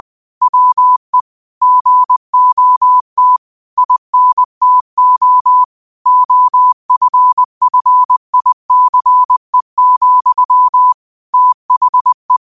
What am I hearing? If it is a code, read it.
Morse code, heard as a keyed tone, about 20 wpm, 1000 Hz. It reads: WE GOT INTO OFFICE, THE